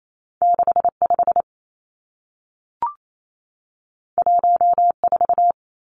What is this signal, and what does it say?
Morse code, heard as a keyed tone, about 28 wpm, 700 Hz. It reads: 6 5 1 4
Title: Two Challenging Letter-Number Combinations 28wpm